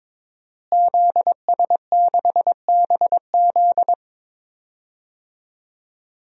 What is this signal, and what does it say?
Morse code, heard as a keyed tone, about 22 wpm, 700 Hz. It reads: ZS6BZ
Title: DXCC-100 Call Signs - 5 Characters - Repeated 2x - 22wpm